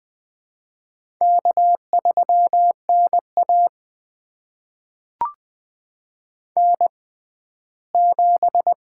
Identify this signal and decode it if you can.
Morse code, heard as a keyed tone, about 20 wpm, spaced as for 14 wpm, 700 Hz. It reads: K3NA N7
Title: USA Call Signs - 6x Character Spacing - 20wpm